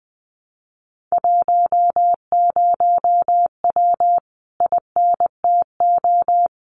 Morse code, heard as a keyed tone, about 20 wpm, 700 Hz. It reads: 10W INTO
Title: Common QSO Phrases Encore 20wpm